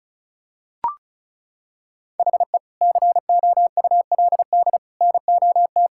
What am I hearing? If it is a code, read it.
Morse code, heard as a keyed tone, about 34 wpm, 700 Hz. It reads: HE COULD NOT
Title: Sentences from Top 1000 Words - Review Part 5 - 35wpm